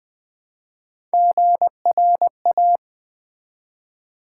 Morse code, heard as a keyed tone, about 20 wpm, 700 Hz. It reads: GRA
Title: Build A Word - 1 Letter At A Time - 3 to 5-Letter Words - 20wpm